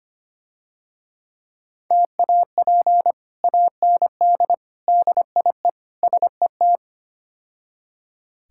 Morse code, heard as a keyed tone, about 28 wpm, 700 Hz. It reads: TAP AND DIE SET
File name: practice / 25wpm / Difficult Words - Speed-Racing with Hints - 25wpm